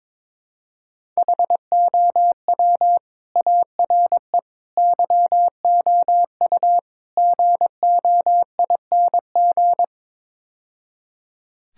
Morse code, heard as a keyed tone, about 21 wpm, 700 Hz. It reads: HOW ARE YOU GOING
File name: practice / 22wpm / Build A Phrase - 5-Word Phrases - 22wpm